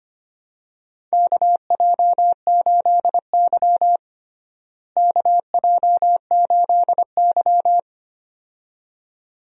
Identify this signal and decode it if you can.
Morse code, heard as a keyed tone, about 25 wpm, 700 Hz. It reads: KJ8Y KJ8Y
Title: USA Call Signs - 4 Characters - Repeated 2x - 25wpm